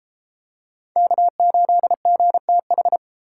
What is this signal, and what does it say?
Morse code, heard as a keyed tone, about 33 wpm, 700 Hz. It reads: K8GTH